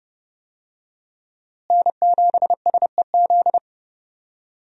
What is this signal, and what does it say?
Morse code, heard as a keyed tone, about 30 wpm, 700 Hz. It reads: N7SEZ
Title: USA Call Signs - 5 Characters - Repeated 2x - 30wpm